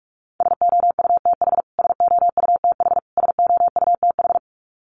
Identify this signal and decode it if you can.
Morse code, heard as a keyed tone, about 44 wpm, 700 Hz. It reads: SOUTH SOUTH SOUTH
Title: Top 500 Words Repeated 45wpm